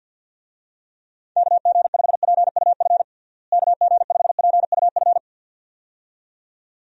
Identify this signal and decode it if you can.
Morse code, heard as a keyed tone, about 54 wpm, 700 Hz. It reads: KG5PUF KG5PUF
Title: USA Call Signs - 6 Characters - Repeated 2x - 50wpm